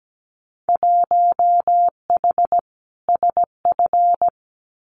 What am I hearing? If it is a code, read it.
Morse code, heard as a keyed tone, about 17 wpm, 700 Hz. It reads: 1H SF
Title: Winter Field Day Exchanges 17wpm